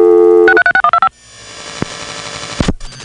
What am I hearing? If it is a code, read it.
DTMF, heard as telephone touch tones, tones of about 80 ms. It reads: 636125